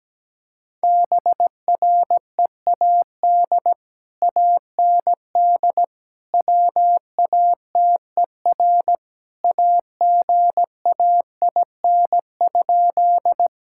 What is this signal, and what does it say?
Morse code, heard as a keyed tone, about 17 wpm, 700 Hz. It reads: BREAD AND WATER AGAIN?
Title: Sentences from Top 1600 Words 17wpm